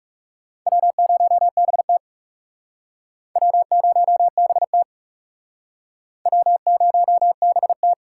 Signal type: Morse code

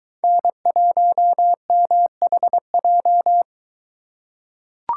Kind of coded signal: Morse code

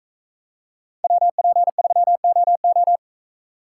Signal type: Morse code

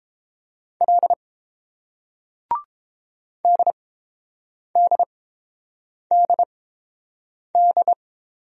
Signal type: Morse code

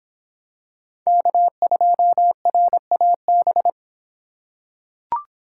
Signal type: Morse code

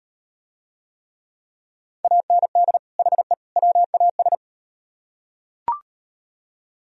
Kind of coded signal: Morse code

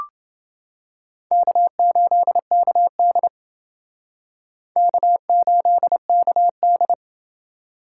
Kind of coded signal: Morse code

